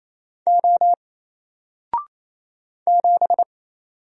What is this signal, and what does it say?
Morse code, heard as a keyed tone, about 27 wpm, 700 Hz. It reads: O 7
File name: Single-Letter-Number-28wpm